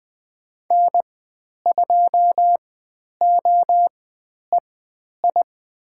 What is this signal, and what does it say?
Morse code, heard as a keyed tone, about 20 wpm, spaced as for 10 wpm, 700 Hz. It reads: N2OEI
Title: 095 Lesson 16g - Letter c - Call Signs 20wpm